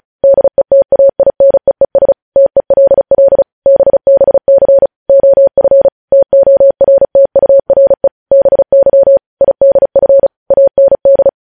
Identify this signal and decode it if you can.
Morse code, heard as a keyed tone, about 36 wpm, 550 Hz. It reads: DETAINEES TELL BBC OF TORTURE BY IDF AND
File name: podcast / News headlines-1x-35x35